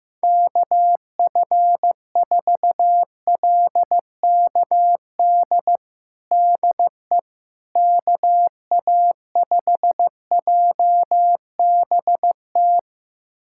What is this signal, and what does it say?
Morse code, heard as a keyed tone, about 15 wpm, 700 Hz. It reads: KF4LKD DE KA5JBT
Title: USA Callsign DE Callsign - 6 Characters - 15wpm